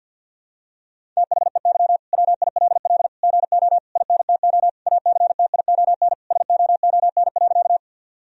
Morse code, heard as a keyed tone, about 50 wpm, 700 Hz. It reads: THEY WILL GO INTO ACTION SOON.
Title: Sentences from Top 1000 Words - Review Part 4 - 50wpm